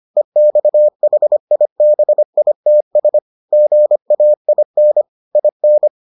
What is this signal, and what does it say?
Morse code, heard as a keyed tone, about 25 wpm, 600 Hz. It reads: EXHIBITS GAIN IN